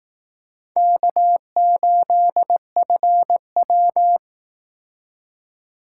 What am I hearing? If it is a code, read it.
Morse code, heard as a keyed tone, about 19 wpm, 700 Hz. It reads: K8FW